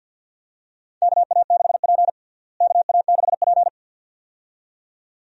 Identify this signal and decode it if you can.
Morse code, heard as a keyed tone, about 49 wpm, 700 Hz. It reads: KA6P KA6P
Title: USA Call Signs - 4 Characters - Repeated 2x - 50wpm